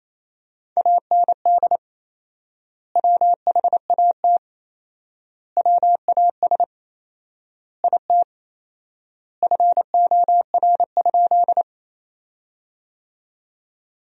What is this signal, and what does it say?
Morse code, heard as a keyed tone, about 28 wpm, 700 Hz. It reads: AND WHAT WAS IT FOR?
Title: 4x Word-Spacing - Sentences from Top 100 Words - 28wpm